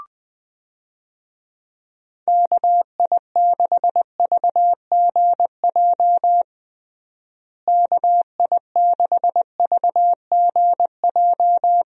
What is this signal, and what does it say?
Morse code, heard as a keyed tone, about 21 wpm, 700 Hz. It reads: KI6VGJ KI6VGJ
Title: USA Call Signs - 6 Characters - Repeated 2x - 20wpm